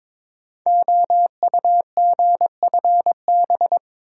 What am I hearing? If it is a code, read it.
Morse code, heard as a keyed tone, about 22 wpm, 700 Hz. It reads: OUGFB
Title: Code Groups - 5 Letters - 22wpm